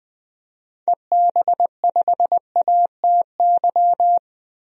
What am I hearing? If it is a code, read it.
Morse code, heard as a keyed tone, about 20 wpm, 700 Hz. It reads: EB5ATY
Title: DXCC-100 Call Signs - 6 Characters - 20wpm